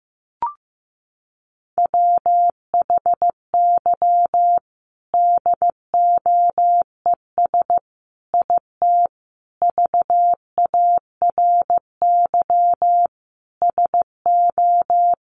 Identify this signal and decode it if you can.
Morse code, heard as a keyed tone, about 15 wpm, 700 Hz. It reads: WHY DOES IT VARY SO
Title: Sentences from Top 1200 Words 15wpm